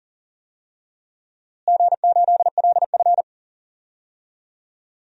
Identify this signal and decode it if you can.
Morse code, heard as a keyed tone, about 40 wpm, 700 Hz. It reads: G8LF
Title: DXCC-100 Call Signs - 4 Characters - Repeated 3x - 40wpm